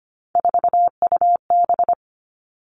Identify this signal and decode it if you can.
Morse code, heard as a keyed tone, about 25 wpm, 700 Hz. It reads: 4UB
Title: Three Challenging Letter-Number Combinations 25wpm